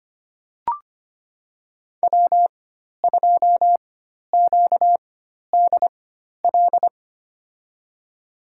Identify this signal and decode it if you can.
Morse code, heard as a keyed tone, about 25 wpm, spaced as for 12 wpm, 700 Hz. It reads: W2QDL
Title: USA Call Signs - 5 Characters - 4x Character Spacing - 25wpm